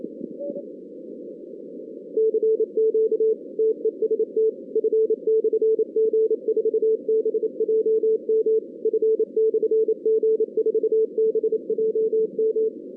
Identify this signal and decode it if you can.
Morse code, heard as a keyed tone, about 28 wpm, 440 Hz. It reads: CQ TEST F/G4BJM F/G4BJM
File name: 2016 arrl dx cw / 2016-ARRL-DX-CW-K5TR- - 0-46